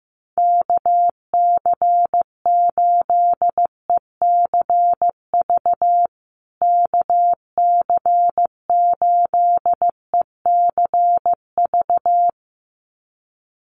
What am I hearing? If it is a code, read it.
Morse code, heard as a keyed tone, about 15 wpm, 700 Hz. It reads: KC8ECV KC8ECV
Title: USA Call Signs Repeated 2x 15wpm